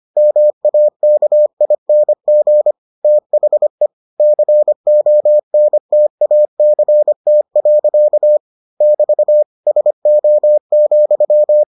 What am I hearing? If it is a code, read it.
Morse code, heard as a keyed tone, about 25 wpm, 600 Hz. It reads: MAKING THE CONTACT. <BT> SO,